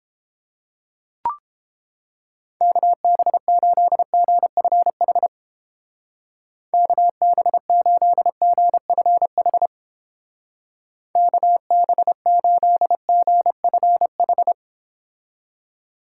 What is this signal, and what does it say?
Morse code, heard as a keyed tone, about 31 wpm, 700 Hz. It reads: KB8GFH KB8GFH KB8GFH